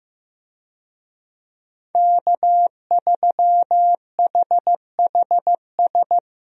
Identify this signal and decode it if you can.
Morse code, heard as a keyed tone, about 15 wpm, 700 Hz. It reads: K3HHS
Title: USA Call Signs - 5 Characters - Repeated 3x - 15wpm